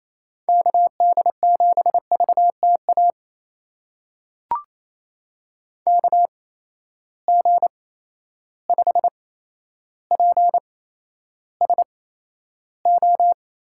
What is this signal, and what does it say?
Morse code, heard as a keyed tone, about 28 wpm, spaced as for 12 wpm, 700 Hz. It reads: KD7VTA KG5PSO
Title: USA Call Signs - 6 Characters - 8x Character Spacing - 28wpm